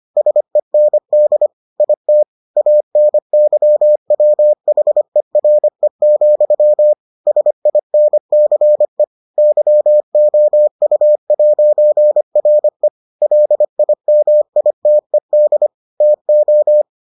Morse code, heard as a keyed tone, about 25 wpm, 600 Hz. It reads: SEND IT ANYWHERE, SINCE YOU'RE LIMITED TO